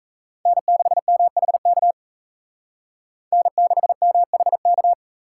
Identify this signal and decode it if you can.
Morse code, heard as a keyed tone, about 36 wpm, 700 Hz. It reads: N6MHK N6MHK